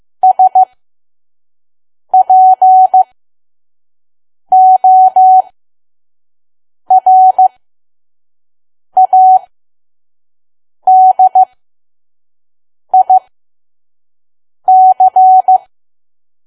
Morse code, heard as a keyed tone, about 15 wpm, spaced as for 5 wpm, 750 Hz. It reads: SPORADIC
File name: Archive / 5 WPM files / PM